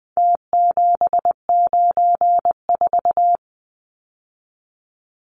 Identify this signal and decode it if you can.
Morse code, heard as a keyed tone, about 20 wpm, 700 Hz. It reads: T794
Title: Numbers - Contest Style - 4-Digit 20wpm